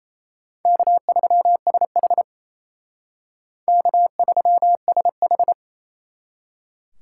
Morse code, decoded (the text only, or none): K3SH K3SH